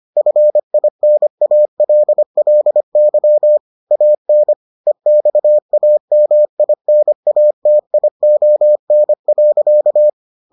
Morse code, heard as a keyed tone, about 25 wpm, 600 Hz. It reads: FINALLY AN EXAMINATION.